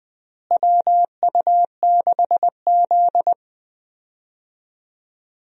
Morse code, heard as a keyed tone, about 20 wpm, 700 Hz. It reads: WU6Z
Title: USA Call Signs - 4 Characters - Repeated 2x - 20wpm